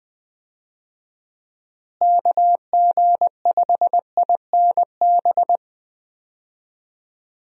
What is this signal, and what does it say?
Morse code, heard as a keyed tone, about 20 wpm, 700 Hz. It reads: KG5INB
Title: USA Call Signs - 6 Characters - Repeated 2x - 20wpm